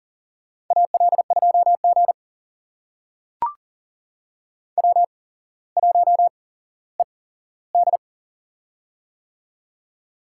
Morse code, heard as a keyed tone, about 38 wpm, spaced as for 17 wpm, 700 Hz. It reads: AL2G W1ED
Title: USA Call Signs - 4 Characters - 8x Character Spacing - 40wpm